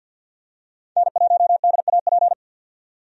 Morse code, heard as a keyed tone, about 53 wpm, 700 Hz. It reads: N1DAP
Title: USA Call Signs - 5 Characters - 50wpm